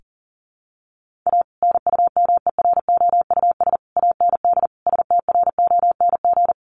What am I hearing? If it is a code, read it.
Morse code, heard as a keyed tone, about 40 wpm, 700 Hz. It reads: A NUMEROUS AND STRONG
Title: Sentences from Top 1400 Words 40wpm